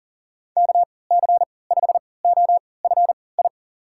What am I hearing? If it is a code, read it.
Morse code, heard as a keyed tone, about 40 wpm, spaced as for 23 wpm, 700 Hz. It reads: KC5OFI
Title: USA Call Signs - 6 Characters - 3x Character Spacing - 40wpm